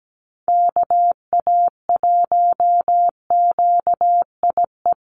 Morse code, heard as a keyed tone, about 17 wpm, 700 Hz. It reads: KA1QIE
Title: USA Call Signs Repeated 3x 17wpm